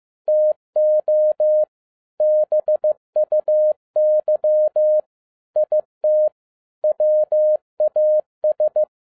Morse code, heard as a keyed tone, about 15 wpm, 610 Hz. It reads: TO BUY IT WAS